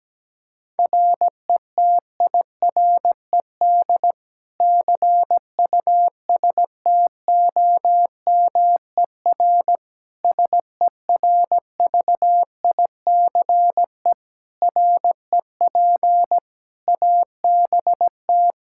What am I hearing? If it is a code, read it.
Morse code, heard as a keyed tone, about 17 wpm, 700 Hz. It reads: RETIRED CUSTOMER SERVICE REP ABT
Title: Practice QSO - Exchange 3 - More Info - 17wpm